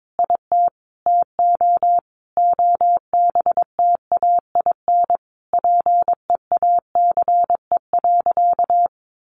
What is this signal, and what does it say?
Morse code, heard as a keyed tone, about 22 wpm, 700 Hz. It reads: IT TO OBTAIN PEACE.
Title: Sentences from Top 1500 Words 22wpm